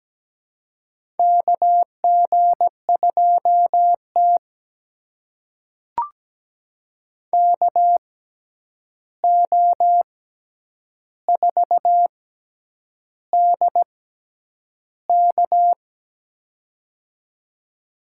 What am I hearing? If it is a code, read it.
Morse code, heard as a keyed tone, about 17 wpm, spaced as for 8 wpm, 700 Hz. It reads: KG2T KO4DK
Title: USA Call Signs - 6x Character Spacing - 17wpm